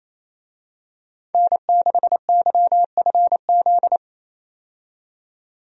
Morse code, heard as a keyed tone, about 28 wpm, 700 Hz. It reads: N6YFZ
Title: USA Call Signs - 5 Characters - 28wpm